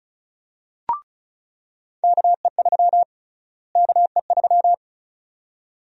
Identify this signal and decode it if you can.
Morse code, heard as a keyed tone, about 36 wpm, 700 Hz. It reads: KE3 KE3
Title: USA Call Sign Prefixes - Repeated 2x - 35wpm